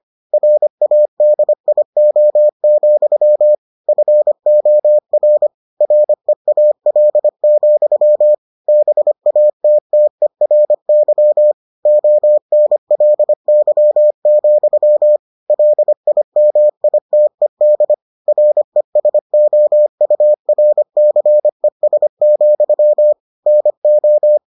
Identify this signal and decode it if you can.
Morse code, heard as a keyed tone, about 25 wpm, 600 Hz. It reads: RADIO, FOR REAL, BATTERY ONLY, LIMITED RESOURCES, NO